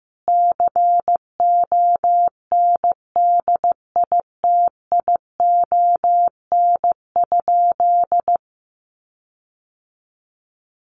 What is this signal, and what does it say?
Morse code, heard as a keyed tone, about 15 wpm, 700 Hz. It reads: CONDITION?
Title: Sentences from Top 1300 Words 15wpm